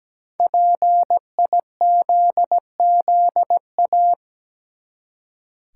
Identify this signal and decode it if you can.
Morse code, heard as a keyed tone, about 17 wpm, 700 Hz. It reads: PIZZA
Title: Five Letter Words - 17wpm